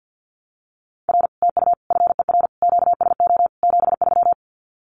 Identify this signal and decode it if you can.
Morse code, heard as a keyed tone, about 50 wpm, 700 Hz. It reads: R TU FER QSO 73
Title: Common QSO Phrases Encore 50wpm